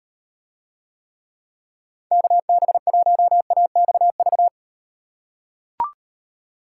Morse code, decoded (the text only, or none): KB1AXV